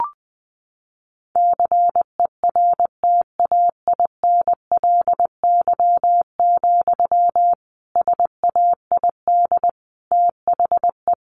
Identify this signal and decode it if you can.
Morse code, heard as a keyed tone, about 20 wpm, 700 Hz. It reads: CERTAINLY, SAID THE
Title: Sentences from Top 1500 Words 20wpm